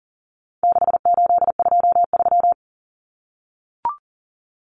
Morse code, heard as a keyed tone, about 40 wpm, 700 Hz. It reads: 6823
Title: Numbers - 4-Digit 40wpm